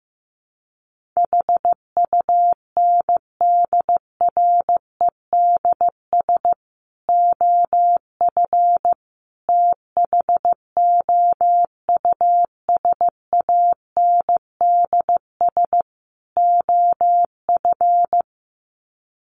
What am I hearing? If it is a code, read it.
Morse code, heard as a keyed tone, about 15 wpm, 700 Hz. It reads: HUNDREDS OF THOUSANDS OF
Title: Sets of 4 Words 15wpm